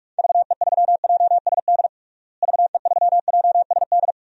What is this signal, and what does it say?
Morse code, heard as a keyed tone, about 46 wpm, 700 Hz. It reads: VE3JSD VE3JSD
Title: DXCC-100 Call Signs - 6 Characters - Repeated 3x - 45wpm